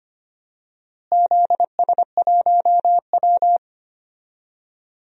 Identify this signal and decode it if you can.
Morse code, heard as a keyed tone, about 25 wpm, 700 Hz. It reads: ZS1W
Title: DXCC-100 Call Signs - 4 Characters - Repeated 2x - 25wpm